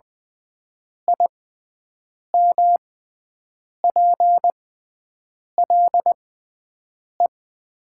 Morse code, heard as a keyed tone, about 20 wpm, spaced as for 7 wpm, 700 Hz. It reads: IMPLE
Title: 6x Character Spacing - Top 500 Words - 20wpm